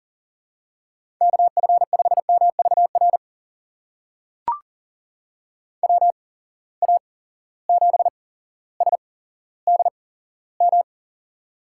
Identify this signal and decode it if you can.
Morse code, heard as a keyed tone, about 40 wpm, spaced as for 17 wpm, 700 Hz. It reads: KF5MVR WA7SDM
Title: USA Call Signs - 6 Characters - 8x Character Spacing - 40wpm